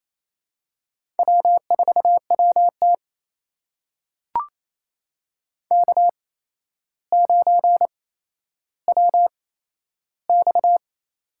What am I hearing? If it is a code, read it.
Morse code, heard as a keyed tone, about 28 wpm, spaced as for 12 wpm, 700 Hz. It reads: W4WT K9WX
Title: USA Call Signs - 4 Characters - 8x Character Spacing - 28wpm